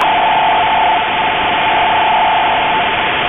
NAVTEX (SITOR-B), about 100 baud. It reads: PERIOD MOVING NE
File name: Radio / SDR Audio / gqrx